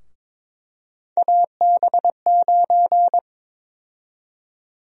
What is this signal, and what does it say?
Morse code, heard as a keyed tone, about 21 wpm, 700 Hz. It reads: AB9
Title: USA Call Sign Prefixes - Repeated 2x - 22wpm